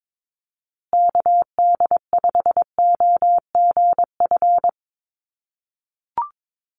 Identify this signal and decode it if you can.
Morse code, heard as a keyed tone, about 22 wpm, 700 Hz. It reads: KD5OGF
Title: USA Call Signs Repeated 2x 22wpm